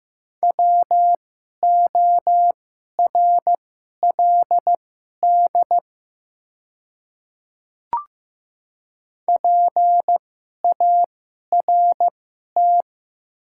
Morse code, heard as a keyed tone, about 15 wpm, spaced as for 11 wpm, 700 Hz. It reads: WORLD PART
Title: 2x Character Spacing - Top 200 Words - 15wpm